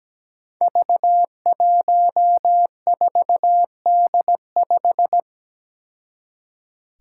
Morse code, heard as a keyed tone, about 17 wpm, 700 Hz. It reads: V14D5
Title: Five Challenging Letter-Number Combinations 17wpm